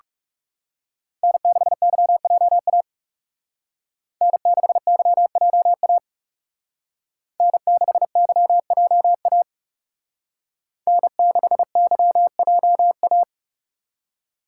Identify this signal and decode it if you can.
Morse code, heard as a keyed tone, about 38 wpm, 700 Hz. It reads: N6YJA N6YJA N6YJA N6YJA